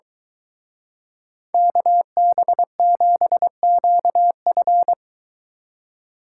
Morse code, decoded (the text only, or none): KB7QF